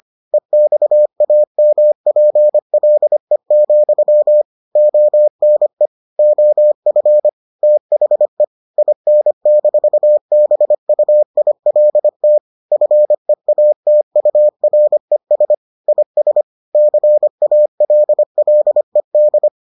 Morse code, heard as a keyed tone, about 25 wpm, 600 Hz. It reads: EXAMPLE, ONE OF THE IN-BUILT FEATURES IS CALLED